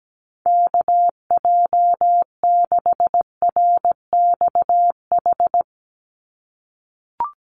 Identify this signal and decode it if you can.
Morse code, heard as a keyed tone, about 17 wpm, 700 Hz. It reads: KJ6RXH
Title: USA Call Signs 17wpm